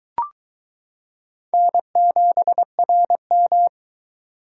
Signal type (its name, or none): Morse code